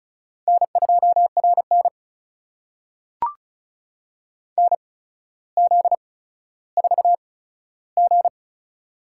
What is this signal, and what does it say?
Morse code, heard as a keyed tone, about 36 wpm, spaced as for 15 wpm, 700 Hz. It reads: N2RN NZ4G
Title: USA Call Signs - 4 Characters - 8x Character Spacing - 35wpm